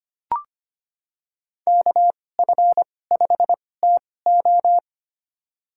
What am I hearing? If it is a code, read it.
Morse code, heard as a keyed tone, about 25 wpm, spaced as for 18 wpm, 700 Hz. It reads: KF5TO
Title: USA Call Signs - 5 Characters - 2x Character Spacing - 25wpm